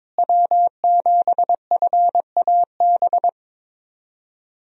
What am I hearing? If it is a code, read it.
Morse code, heard as a keyed tone, about 22 wpm, 700 Hz. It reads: W7FAB
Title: USA Call Signs - 5 Characters - Repeated 3x - 22wpm